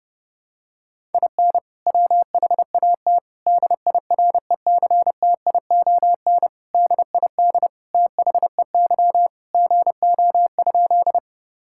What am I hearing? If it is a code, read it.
Morse code, heard as a keyed tone, about 32 wpm, 700 Hz. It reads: IN WHAT DIRECTION DID THEY GO?
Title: Sentences from Top 1000 Words - Review Part 5 - 30wpm